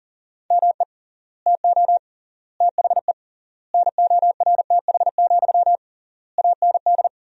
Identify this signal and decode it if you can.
Morse code, heard as a keyed tone, about 37 wpm, 700 Hz. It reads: ME TO THE NORTH, AND
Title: 3x Word-Spacing - Sentences from Top 400 Words - 40wpm